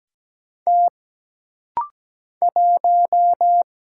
Morse code, heard as a keyed tone, about 17 wpm, 700 Hz. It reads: T 1
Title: Single Letter-Number Alt - Rapid-Fire 17wpm